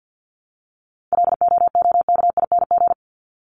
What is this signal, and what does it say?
Morse code, heard as a keyed tone, about 48 wpm, 700 Hz. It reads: LOOKING
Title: ING Suffix - Top 100 Words with Base Words 50wpm